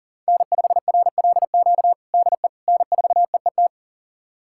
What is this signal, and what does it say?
Morse code, heard as a keyed tone, about 40 wpm, 700 Hz. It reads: N5RLQ DE N4EET
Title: USA Callsign DE Callsign - 5 Characters - 40wpm